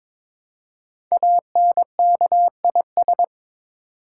Morse code, heard as a keyed tone, about 22 wpm, 690 Hz. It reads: ANKIS
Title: Code Groups - 5 Letters - 22wpm